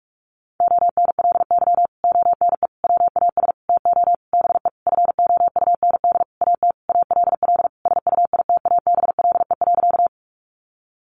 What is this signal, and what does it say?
Morse code, heard as a keyed tone, about 44 wpm, 700 Hz. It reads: ONLY ONE WAS TO BE FOUND AT ALL SUITABLE.
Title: Sentences from Top 1400 Words 45wpm